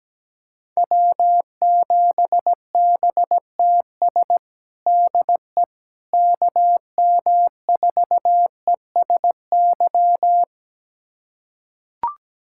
Morse code, heard as a keyed tone, about 17 wpm, 700 Hz. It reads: W7BTS DE KM4ESY
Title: USA Callsign DE Callsign 17wpm